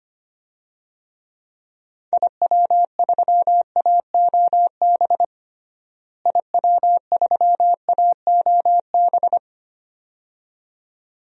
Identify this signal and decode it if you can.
Morse code, heard as a keyed tone, about 25 wpm, 700 Hz. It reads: IW3AOB IW3AOB
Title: DXCC-100 Call Signs - 6 Characters - Repeated 2x - 25wpm